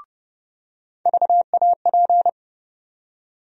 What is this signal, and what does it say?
Morse code, heard as a keyed tone, about 32 wpm, 700 Hz. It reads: VAP
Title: USA Call Sign Postfixes - 3 Letters 30wpm